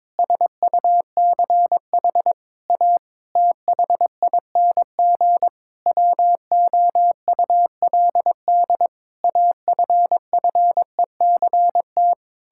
Morse code, heard as a keyed tone, about 22 wpm, 700 Hz. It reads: SUCH A THING WOULD AFFECT
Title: Sentences from Top 1000 Words - Review Part 6 - 22wpm